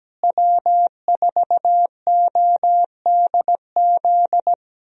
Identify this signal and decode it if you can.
Morse code, heard as a keyed tone, about 17 wpm, 700 Hz. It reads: W4ODZ
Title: USA Call Signs - 5 Characters - Repeated 3x - 17wpm